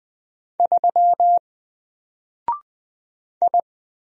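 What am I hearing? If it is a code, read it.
Morse code, heard as a keyed tone, about 20 wpm, 700 Hz. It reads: 3 I
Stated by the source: Single letter and number Morse drill at 20 wpm, from N4LCD.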